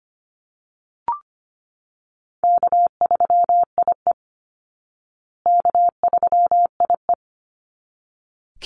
Morse code, heard as a keyed tone, about 25 wpm, 700 Hz. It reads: K3IE K3IE
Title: Easy Call Signs Repeated 2x - 25wpm